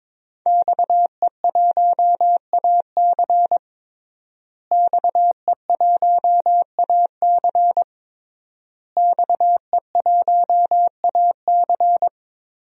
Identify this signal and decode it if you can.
Morse code, heard as a keyed tone, about 22 wpm, 700 Hz. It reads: XE1AC XE1AC XE1AC
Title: DXCC-100 Call Signs - 5 Characters - Repeated 3x - 22wpm